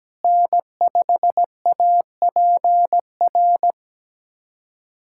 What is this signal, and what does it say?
Morse code, heard as a keyed tone, about 17 wpm, 700 Hz. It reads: N5APR
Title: USA Callsign DE Callsign 17wpm